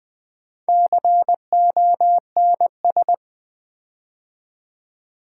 Morse code, heard as a keyed tone, about 20 wpm, 700 Hz. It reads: CONS
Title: Build A Word - 4 Letters At A Time - 11-Letter Words - 20wpm